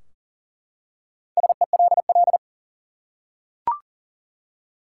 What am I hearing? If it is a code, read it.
Morse code, heard as a keyed tone, about 40 wpm, 700 Hz. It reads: SELL
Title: Four Letter Words - 40wpm